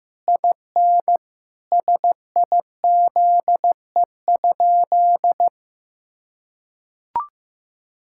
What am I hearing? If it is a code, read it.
Morse code, heard as a keyed tone, about 15 wpm, 700 Hz. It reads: IN SIZE?
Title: Sentences from Top 1000 Words - Review Part 5 - 15wpm